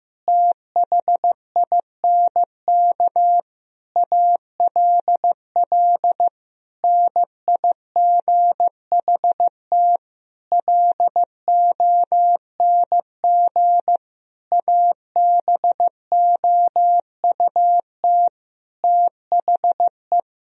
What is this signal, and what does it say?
Morse code, heard as a keyed tone, about 15 wpm, 700 Hz. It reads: THINK ALL NIGHT LONG ABOUT THE
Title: Sentences from Top 1000 Words - Review Part 3 - 15wpm